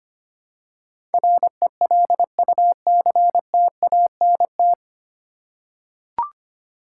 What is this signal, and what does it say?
Morse code, heard as a keyed tone, about 25 wpm, 700 Hz. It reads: RELUCTANT
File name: Nine Letter Words - 25wpm